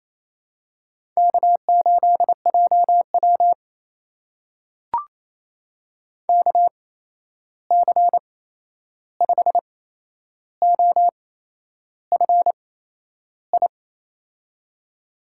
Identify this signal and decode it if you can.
Morse code, heard as a keyed tone, about 28 wpm, spaced as for 11 wpm, 700 Hz. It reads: K8JW KC5OFI
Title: USA Call Signs - 8x Character Spacing - 28wpm